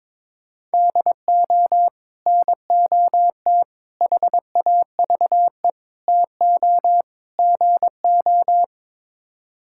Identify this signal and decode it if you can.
Morse code, heard as a keyed tone, about 22 wpm, 700 Hz. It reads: DO NOT HAVE TO GO
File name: practice / 22wpm / Build A Phrase - 5-Word Phrases - 22wpm